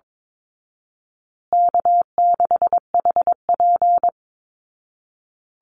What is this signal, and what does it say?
Morse code, heard as a keyed tone, about 22 wpm, 700 Hz. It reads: K6HP
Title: Call Signs Repeated 2x - 22wpm